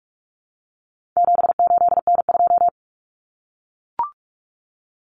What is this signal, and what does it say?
Morse code, heard as a keyed tone, about 44 wpm, 700 Hz. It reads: 78N2
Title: Common Cut Numbers - 4-Digit 45wpm